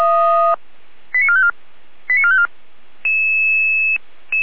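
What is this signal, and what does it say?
Five-tone selective call (ZVEI), heard as quick runs of five tones, tones of about 70 ms. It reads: 89454 89454
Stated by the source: ZVEI five-tone selective call received on SDR, from the feuerwehr-orlamuende.org (fire brigade Orlamünde) audio server.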